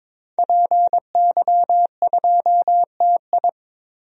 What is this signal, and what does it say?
Morse code, heard as a keyed tone, about 22 wpm, 700 Hz. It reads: PY2TI
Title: DXCC-100 Call Signs - 5 Characters - 22wpm